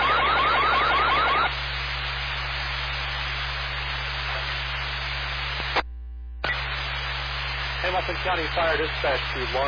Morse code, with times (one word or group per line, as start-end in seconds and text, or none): none